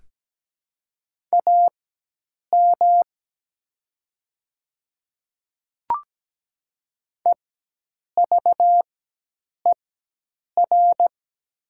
Morse code, heard as a keyed tone, about 17 wpm, spaced as for 8 wpm, 700 Hz. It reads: AM EVER
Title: 4x Character Spacing - Top 500 Words - 17wpm